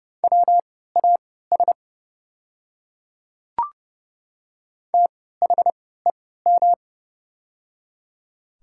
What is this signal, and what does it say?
Morse code, heard as a keyed tone, about 31 wpm, spaced as for 17 wpm, 700 Hz. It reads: WAS THEM
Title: 3x Character Spacing - Top 200 Words - 30wpm